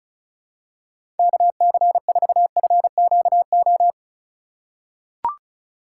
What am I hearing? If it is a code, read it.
Morse code, heard as a keyed tone, about 34 wpm, 700 Hz. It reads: KC4FQO
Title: USA Call Signs - 3x Character Spacing - 35wpm